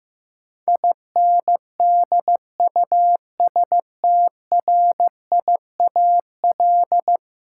Morse code, heard as a keyed tone, about 15 wpm, 700 Hz. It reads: INDUSTRIAL
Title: Sets of 5 Words - New Word Drill - 15wpm